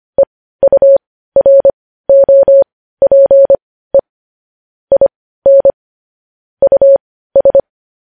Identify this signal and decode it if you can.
Morse code, heard as a keyed tone, about 25 wpm, spaced as for 15 wpm, 550 Hz. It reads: EUROPE IN US